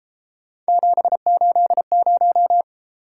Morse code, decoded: 780

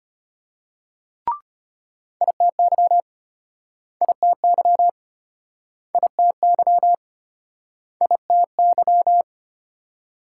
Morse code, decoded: ITY ITY ITY ITY

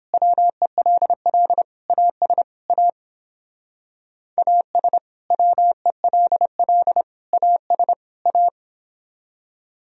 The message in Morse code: WELL AS A AS WELL AS A

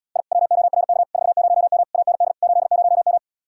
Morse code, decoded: I WORK 40M AND 80M

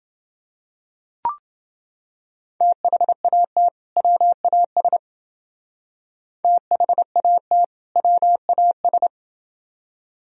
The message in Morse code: THAT WAS THAT WAS

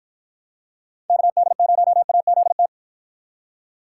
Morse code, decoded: KD0AZT

